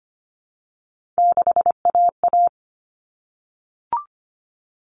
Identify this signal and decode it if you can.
Morse code, heard as a keyed tone, about 25 wpm, 700 Hz. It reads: 6AA
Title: Common Cut Numbers - 3-Digit 25wpm